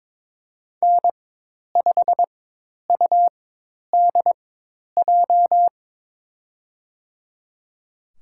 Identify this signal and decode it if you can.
Morse code, heard as a keyed tone, about 22 wpm, spaced as for 10 wpm, 700 Hz. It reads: N5UDJ
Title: USA Call Signs - 5 Characters - 4x Character Spacing - 22wpm